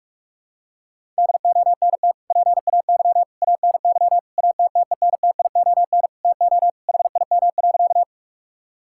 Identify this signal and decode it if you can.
Morse code, heard as a keyed tone, about 41 wpm, 700 Hz. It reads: DONT PAY ANY ATTENTION TO HIM.
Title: Sentences from Top 1000 Words - Review Part 6 - 45wpm